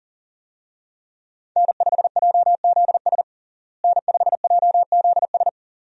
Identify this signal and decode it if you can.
Morse code, heard as a keyed tone, about 40 wpm, 700 Hz. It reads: N5JZS N5JZS
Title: USA Call Signs - 5 Characters - Repeated 2x - 40wpm